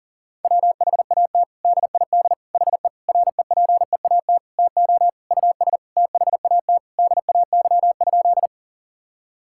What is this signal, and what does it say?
Morse code, heard as a keyed tone, about 37 wpm, 700 Hz. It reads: WHAT DID HE REPEAT TO US THAT DAY?
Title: Sentences from Top 1600 Words 40wpm